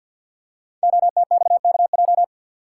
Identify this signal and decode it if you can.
Morse code, heard as a keyed tone, about 52 wpm, 700 Hz. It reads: OTXKJ
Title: Code Groups - 5 Letter:Numbers - 50wpm